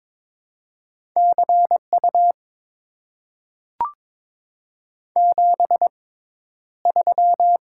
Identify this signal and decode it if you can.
Morse code, heard as a keyed tone, about 22 wpm, spaced as for 11 wpm, 700 Hz. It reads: CU 73
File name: Top 90 QSO Elements - 6x Character Spacing - 22wpm